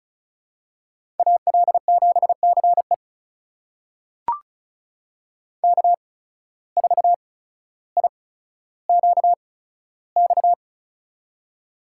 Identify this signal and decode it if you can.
Morse code, heard as a keyed tone, about 36 wpm, spaced as for 15 wpm, 700 Hz. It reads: AL7CE K4IQX
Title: USA Call Signs - 5 Characters - 8x Character Spacing - 35wpm